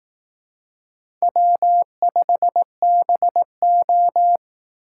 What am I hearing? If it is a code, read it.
Morse code, heard as a keyed tone, about 18 wpm, 700 Hz. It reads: W5BO
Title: Speed Racing - USA 1x2 Call Signs 15wpm